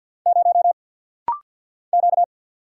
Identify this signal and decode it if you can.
Morse code, heard as a keyed tone, about 50 wpm, 700 Hz. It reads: 0 Q
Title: Single Letter-Number - Rapid-Fire 50wpm